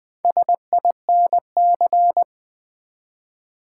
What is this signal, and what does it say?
Morse code, heard as a keyed tone, about 20 wpm, 700 Hz. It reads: SINC
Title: Build A Word - 1 Letter At A Time - 5-Letter Words - 20wpm